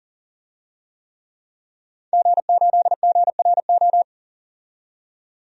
Morse code, heard as a keyed tone, about 40 wpm, 700 Hz. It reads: G8GRO
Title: DXCC-100 Call Signs - 5 Characters - 40wpm